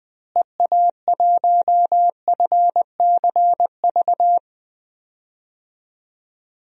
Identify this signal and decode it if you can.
Morse code, heard as a keyed tone, about 20 wpm, 700 Hz. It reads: EA1FCV
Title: DXCC-100 Call Signs - 6 Characters - Repeated 2x - 20wpm